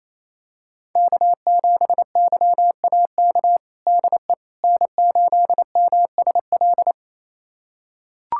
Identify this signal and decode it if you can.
Morse code, heard as a keyed tone, about 27 wpm, 700 Hz. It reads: K7YAK DE N8MSL
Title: USA Callsign DE Callsign - 5 Characters - 28wpm